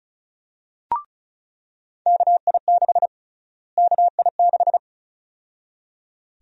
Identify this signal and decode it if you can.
Morse code, heard as a keyed tone, about 34 wpm, 700 Hz. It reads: KI6 KI6
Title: USA Call Sign Prefixes - Repeated 2x - 35wpm